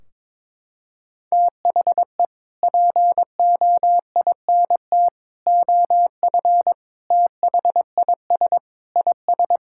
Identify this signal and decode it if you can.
Morse code, heard as a keyed tone, about 22 wpm, 700 Hz. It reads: THE POINT OF THIS IS
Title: Sentences from Top 1600 Words 22wpm